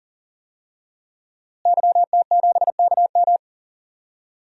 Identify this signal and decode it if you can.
Morse code, heard as a keyed tone, about 38 wpm, 700 Hz. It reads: YT7KM
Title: DXCC-100 Call Signs - 5 Characters - Repeated 2x - 40wpm